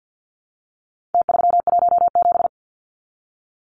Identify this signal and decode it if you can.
Morse code, heard as a keyed tone, about 50 wpm, 700 Hz. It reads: T317
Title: Numbers - Contest Style - 4-Digit 50wpm